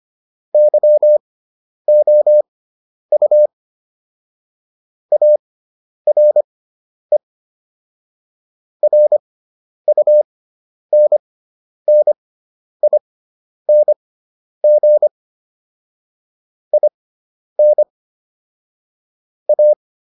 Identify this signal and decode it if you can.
Morse code, heard as a keyed tone, about 25 wpm, spaced as for 10 wpm, 600 Hz. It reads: YOU ARE RUNNING IN A